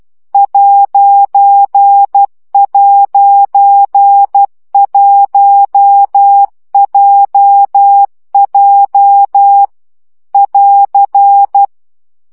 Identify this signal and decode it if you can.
Morse code, heard as a keyed tone, about 12 wpm, 800 Hz. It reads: ''1JJ <AR>
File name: CW gyakorlas j 1 aposztrof 10 baud